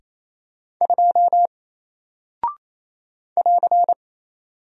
Morse code, heard as a keyed tone, about 28 wpm, 700 Hz. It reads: 2 <AR>
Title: Single Character (Letter:Number:Punctuation:Prosign) Alt 28wpm